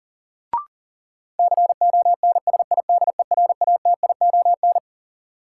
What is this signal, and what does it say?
Morse code, heard as a keyed tone, about 40 wpm, 700 Hz. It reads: CONSIDERATION
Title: Sentences from Top 1500 Words - New Word Drill Alt - 40wpm